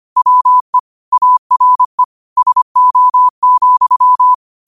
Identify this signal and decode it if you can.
Morse code, heard as a keyed tone, about 25 wpm, 1000 Hz. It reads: WE ARE SO,
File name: qotdfiles / 25wpm